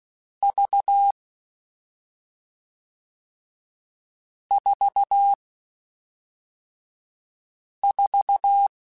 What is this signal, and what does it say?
Morse code, heard as a keyed tone, about 16 wpm, 800 Hz. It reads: V 4 4